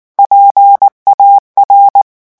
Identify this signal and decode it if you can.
Morse code, heard as a keyed tone, about 19 wpm, 800 Hz. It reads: PAR
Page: REVISION 8 - 8 mots minute